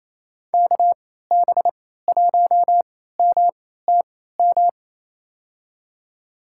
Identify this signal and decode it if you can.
Morse code, heard as a keyed tone, about 28 wpm, spaced as for 16 wpm, 700 Hz. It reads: KB1MTM
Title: USA Call Signs - 3x Character Spacing - 28wpm